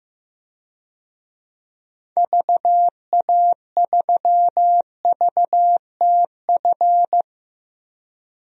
Morse code, heard as a keyed tone, about 16 wpm, 700 Hz. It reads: VA3VTF
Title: DXCC-100 Call Signs - 6 Characters - Repeated 2x - 15wpm